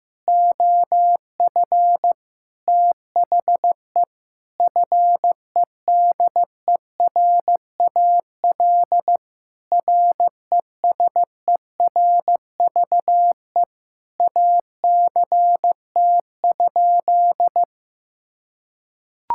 Morse code, heard as a keyed tone, about 16 wpm, 700 Hz. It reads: OF THE FEDERAL RESERVE ACT?
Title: Sentences from Top 1000 Words - Review Part 6 - 15wpm